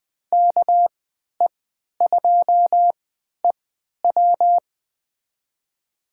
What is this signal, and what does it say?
Morse code, heard as a keyed tone, about 20 wpm, spaced as for 11 wpm, 700 Hz. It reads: KE2EW
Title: USA Call Signs - 5 Characters - 3x Character Spacing - 20wpm